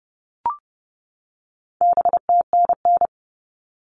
Morse code, heard as a keyed tone, about 30 wpm, 700 Hz. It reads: BTNN
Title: All Cut Numbers - 4-Digit 30wpm